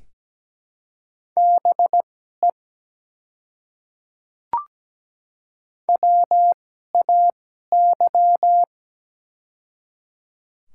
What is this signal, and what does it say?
Morse code, heard as a keyed tone, about 17 wpm, spaced as for 12 wpm, 700 Hz. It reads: BE WAY
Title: 2x Character Spacing - Top 100 Words - 17wpm